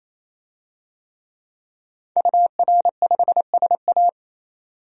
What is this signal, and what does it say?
Morse code, heard as a keyed tone, about 27 wpm, 700 Hz. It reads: UR5SA
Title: DXCC-100 Call Signs - 5 Characters - 28wpm